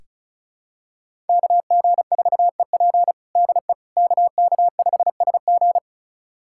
Morse code, heard as a keyed tone, about 36 wpm, 700 Hz. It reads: KG4EP DE KK5SG
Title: USA Callsign DE Callsign - 5 Characters - 35wpm